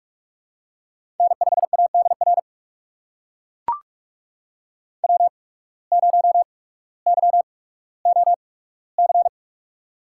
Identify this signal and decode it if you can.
Morse code, heard as a keyed tone, about 44 wpm, spaced as for 19 wpm, 690 Hz. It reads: N5ADR W0YOC
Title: USA Call Signs - 5 Characters - 8x Character Spacing - 45wpm